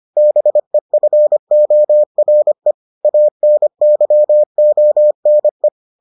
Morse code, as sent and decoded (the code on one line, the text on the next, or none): -... ..-. --- .-. / .- -. -.-- --- -. .
BEFORE ANYONE